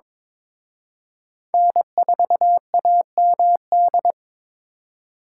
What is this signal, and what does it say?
Morse code, heard as a keyed tone, about 26 wpm, 700 Hz. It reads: N4AMD
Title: Speed Racing - USA Call Signs 22wpm